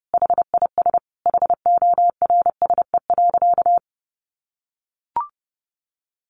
Morse code, heard as a keyed tone, about 30 wpm, 700 Hz. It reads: HIS HORSE.
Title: Sentences from Top 1100 Words 30wpm